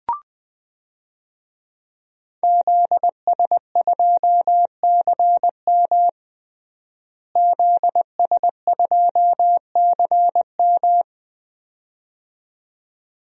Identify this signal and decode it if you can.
Morse code, heard as a keyed tone, about 20 wpm, 700 Hz. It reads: ZS2CM ZS2CM
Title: DXCC-100 Call Signs - 5 Characters - Repeated 2x - 20wpm